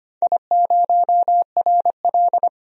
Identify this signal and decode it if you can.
Morse code, heard as a keyed tone, about 25 wpm, 700 Hz. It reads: I0RL
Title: DXCC-100 Call Signs - 4 Characters - 25wpm